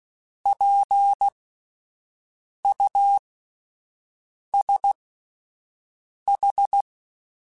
Morse code, heard as a keyed tone, about 16 wpm, spaced as for 5 wpm, 800 Hz. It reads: PUSH